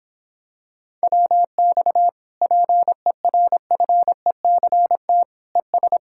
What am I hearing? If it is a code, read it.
Morse code, heard as a keyed tone, about 26 wpm, 700 Hz. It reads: WX PERFECT ES